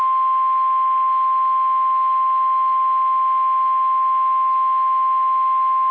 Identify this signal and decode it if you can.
Emergency Alert System (SAME) tones (the 1050 Hz attention tone). It is heard from 0.7 to 5.7 s.
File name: eas / KSDZKDJL / KSDZKDJL